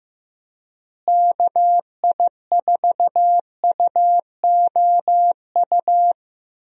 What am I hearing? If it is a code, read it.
Morse code, heard as a keyed tone, about 15 wpm, 700 Hz. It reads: KI4UOU
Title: USA Call Signs - 6 Characters - Repeated 3x - 15wpm